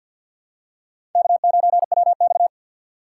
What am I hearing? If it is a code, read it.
Morse code, heard as a keyed tone, about 52 wpm, 700 Hz. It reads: K9WX
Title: USA Call Signs - 4 Characters - 3x Character Spacing - 50wpm